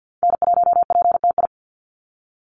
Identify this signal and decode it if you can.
Morse code, heard as a keyed tone, about 48 wpm, 700 Hz. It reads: N1PTI
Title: USA Call Signs 50wpm